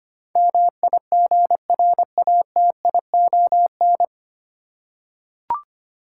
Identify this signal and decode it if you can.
Morse code, heard as a keyed tone, about 25 wpm, 700 Hz. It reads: MIGRATION
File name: practice / 25wpm / 9 Letter Words - First 3 Letters Given - 25wpm